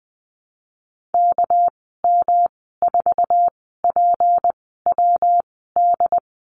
Morse code, heard as a keyed tone, about 20 wpm, spaced as for 14 wpm, 700 Hz. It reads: KM4PWD
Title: 219 Lesson 35g - Number 0 - Call Signs 20wpm